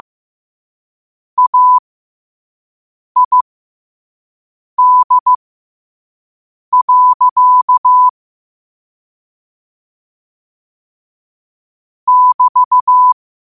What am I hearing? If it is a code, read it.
Morse code, heard as a keyed tone, about 15 wpm, spaced as for 5 wpm, 1000 Hz. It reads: AID. <BT>
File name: qotdfiles / 5wpm